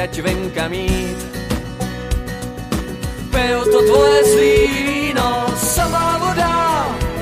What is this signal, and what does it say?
Ringback tone (Europe, 425 Hz), with noise.